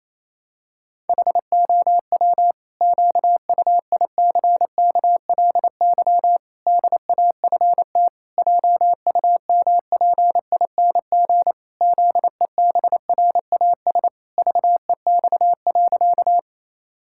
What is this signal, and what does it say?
Morse code, heard as a keyed tone, about 27 wpm, 700 Hz. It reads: HOW QUICKLY DAFT JUMPING ZEBRAS VEX.
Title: Pangram Challenge v2 - 28wpm